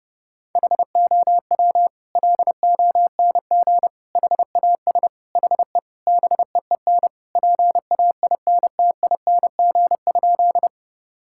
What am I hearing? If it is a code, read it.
Morse code, heard as a keyed tone, about 30 wpm, 700 Hz. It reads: HOW LONG HAS HE BEEN PAINTING?
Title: Sentences from Top 1000 Words - Review Part 6 - 30wpm